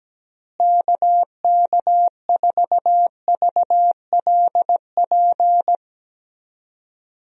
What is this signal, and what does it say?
Morse code, heard as a keyed tone, about 17 wpm, 700 Hz. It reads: KK4VLP
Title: USA Call Signs - 6 Characters - Repeated 3x - 17wpm